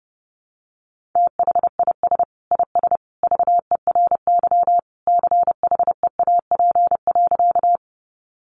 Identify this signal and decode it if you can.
Morse code, heard as a keyed tone, about 30 wpm, 700 Hz. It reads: THIS IS VERY CHEAP.
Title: Sentences from Top 1200 Words 30wpm